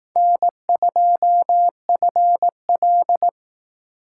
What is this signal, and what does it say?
Morse code, heard as a keyed tone, about 18 wpm, 700 Hz. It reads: N2FL
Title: Speed Racing - USA 1x2 Call Signs 15wpm